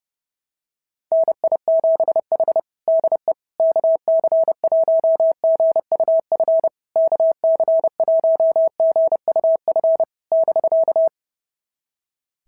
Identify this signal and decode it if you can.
Morse code, heard as a keyed tone, about 29 wpm, 690 Hz. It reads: NI7H DE KC1GUF KC1GUF <BK>
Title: SOTA QSO Practice 30wpm